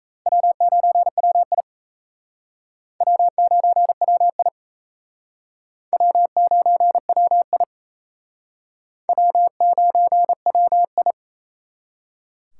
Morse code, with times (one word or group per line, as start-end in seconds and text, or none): 0.3-1.6 W9WI
3.0-4.5 W9WI
5.9-7.6 W9WI
9.1-11.1 W9WI